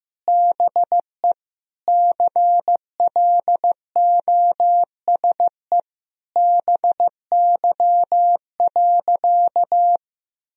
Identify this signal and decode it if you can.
Morse code, heard as a keyed tone, about 15 wpm, 700 Hz. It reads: BE CLOSE BY.
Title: Sentences from Top 1000 Words - Review Part 1 - 15wpm